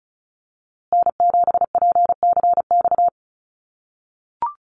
Morse code, heard as a keyed tone, about 34 wpm, 700 Hz. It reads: N7PCX
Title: USA Call Signs Repeated 3x 35wpm